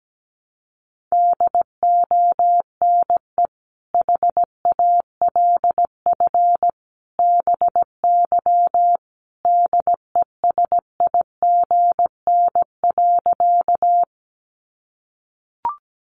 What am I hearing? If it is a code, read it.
Morse code, heard as a keyed tone, about 17 wpm, 700 Hz. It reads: DONE HALF BY DESIGN.
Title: Sentences from Top 800 Words 17wpm